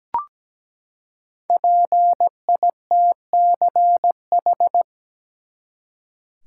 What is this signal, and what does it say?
Morse code, heard as a keyed tone, about 17 wpm, 700 Hz. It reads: PITCH
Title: Five Letter Words - 17wpm